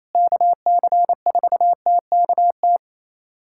Morse code, heard as a keyed tone, about 28 wpm, 700 Hz. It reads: KC4TKT
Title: USA Callsign DE Callsign - 6 Characters - 28wpm